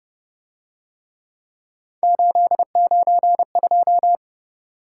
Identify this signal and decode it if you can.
Morse code, heard as a keyed tone, about 28 wpm, 700 Hz. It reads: 892